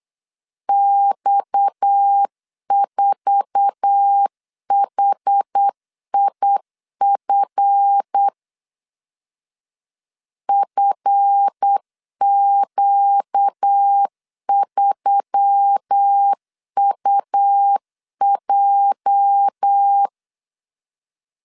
Morse code, heard as a keyed tone, about 8 wpm, 800 Hz. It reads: X4HIF FQ3UJ
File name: LEZIONI / 8 YJ45 / 8 YJ4 8wpm 3246